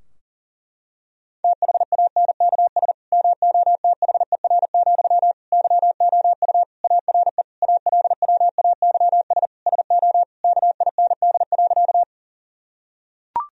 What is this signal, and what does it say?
Morse code, heard as a keyed tone, about 40 wpm, 700 Hz. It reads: THANKS MOTHER, YOU ARE ALWAYS SO KIND.
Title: Sentences from Top 1000 Words - Review Part 3 - 40wpm